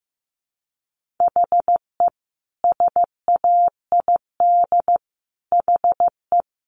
Morse code, heard as a keyed tone, about 15 wpm, 700 Hz. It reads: HE SAID HE
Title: Sentences from Top 1500 Words 15wpm